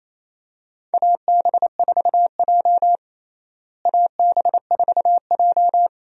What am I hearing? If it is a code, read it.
Morse code, heard as a keyed tone, about 28 wpm, 700 Hz. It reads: AB4J AB4J
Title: USA Call Signs - 4 Characters - Repeated 2x - 28wpm